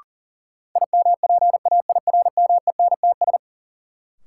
Morse code, heard as a keyed tone, about 40 wpm, 700 Hz. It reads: IMPAIRMENTS
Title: Sets of 5 Words Encore - New Word Drill - 40wpm